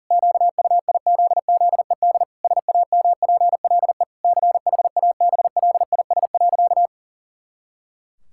Morse code, heard as a keyed tone, about 40 wpm, 700 Hz. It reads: QUIZZED SAMPLE CHABLIS.
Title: Pangram Challenge v2 - 40wpm